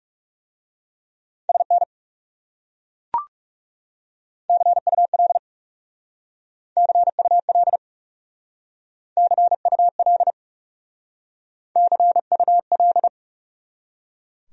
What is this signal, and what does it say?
Morse code, heard as a keyed tone, about 40 wpm, 700 Hz. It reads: SN CUL CUL CUL CUL